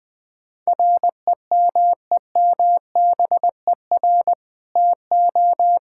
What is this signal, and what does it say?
Morse code, heard as a keyed tone, about 20 wpm, 700 Hz. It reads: REMEMBER TO
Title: Sentences with 2-Word Phrases - 20wpm